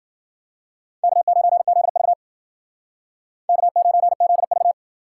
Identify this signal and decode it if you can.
Morse code, heard as a keyed tone, about 51 wpm, 700 Hz. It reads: K9ZV K9ZV